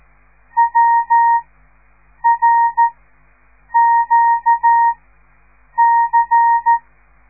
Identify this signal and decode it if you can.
Morse code, heard as a keyed tone, about 13 wpm, spaced as for 7 wpm, 930 Hz. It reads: WRQC
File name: radiomorningstar / morse / les 5